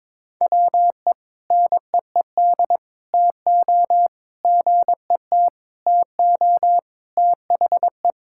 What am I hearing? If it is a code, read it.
Morse code, heard as a keyed tone, about 22 wpm, 700 Hz. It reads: WE NEED TO GET TO THE
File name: practice / 22wpm / Sentences with 3-Word Phrases - 22wpm